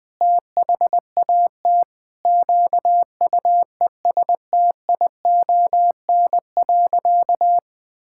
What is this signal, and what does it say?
Morse code, heard as a keyed tone, about 21 wpm, 700 Hz. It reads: THAT QUESTION.
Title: Sentences with 7-Word Phrases - 20wpm